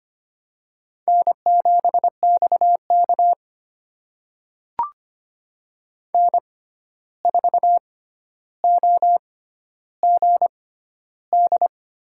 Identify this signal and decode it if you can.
Morse code, heard as a keyed tone, about 25 wpm, spaced as for 12 wpm, 700 Hz. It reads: N7XK N4OGD
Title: USA Call Signs - 6x Character Spacing - 25wpm